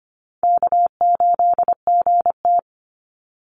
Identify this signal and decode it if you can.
Morse code, heard as a keyed tone, about 25 wpm, 700 Hz. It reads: K8GT
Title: Call Signs - 25wpm